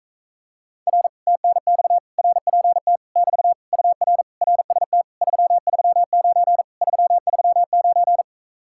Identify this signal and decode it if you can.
Morse code, heard as a keyed tone, about 45 wpm, 700 Hz. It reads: R TNX RPT <BT> UR RST 339 339